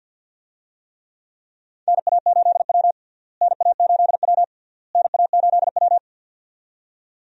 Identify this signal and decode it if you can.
Morse code, heard as a keyed tone, about 47 wpm, 700 Hz. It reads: NA8W NA8W NA8W
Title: USA Call Signs - 4 Characters - Repeated 3x - 50wpm